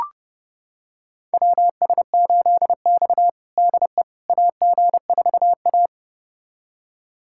Morse code, heard as a keyed tone, about 30 wpm, 700 Hz. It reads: WS8X DE AG4A
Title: USA Callsign DE Callsign - 4 Characters - 30wpm